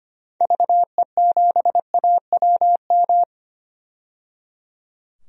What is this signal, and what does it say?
Morse code, heard as a keyed tone, about 25 wpm, 700 Hz. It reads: VE7AWM
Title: DXCC-100 Call Signs - 6 Characters - 25wpm